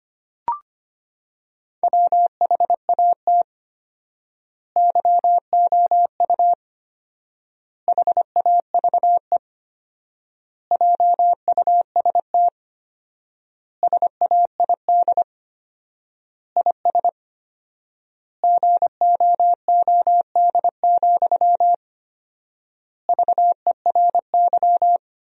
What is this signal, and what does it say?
Morse code, heard as a keyed tone, about 25 wpm, 700 Hz. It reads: WHAT YOU HAVE JUST SAID IS GOOD, VERY
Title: 4x Word-Spacing - Sentences from Top 200 Words - 25wpm